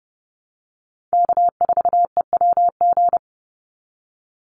Morse code, heard as a keyed tone, about 30 wpm, 700 Hz. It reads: K4EWG
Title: Call Signs Repeated 3x - 30wpm